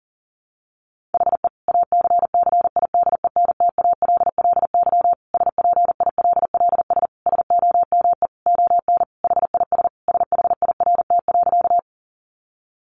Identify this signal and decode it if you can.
Morse code, heard as a keyed tone, about 40 wpm, 700 Hz. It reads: HE ACCIDENTALLY SPILLS SOME ON HIS SHIRT.
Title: Additional Easy Phrases 40wpm